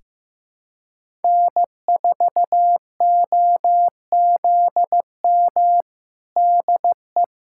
Morse code, heard as a keyed tone, about 15 wpm, 700 Hz. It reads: N4OZM DE
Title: USA Callsign DE Callsign - 5 Characters - 15wpm